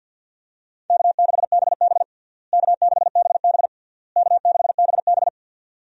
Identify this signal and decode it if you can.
Morse code, heard as a keyed tone, about 50 wpm, 700 Hz. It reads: K6BB K6BB K6BB
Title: USA Call Signs - 4 Characters - Repeated 3x - 50wpm